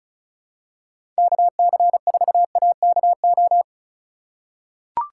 Morse code, heard as a keyed tone, about 36 wpm, 700 Hz. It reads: KC4AKO
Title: USA Call Signs - 6 Characters - 4x Character Spacing - 35wpm